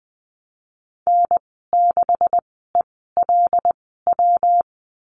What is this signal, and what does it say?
Morse code, heard as a keyed tone, about 20 wpm, spaced as for 15 wpm, 700 Hz. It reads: N6ELW
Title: 179 Lesson 29g - Slash - Call Signs 20wpm